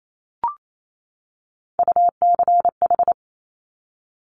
Morse code, heard as a keyed tone, about 28 wpm, 700 Hz. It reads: UCH
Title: Top Three Letter Combinations in Words 28wpm